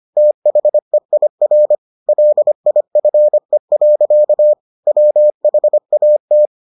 Morse code, heard as a keyed tone, about 25 wpm, 600 Hz. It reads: THEIR LIFE. WHAT